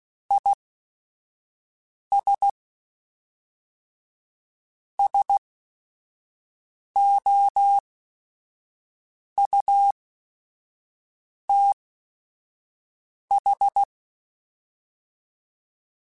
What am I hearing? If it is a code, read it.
Morse code, heard as a keyed tone, about 16 wpm, spaced as for 5 wpm, 800 Hz. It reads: IS SOUTH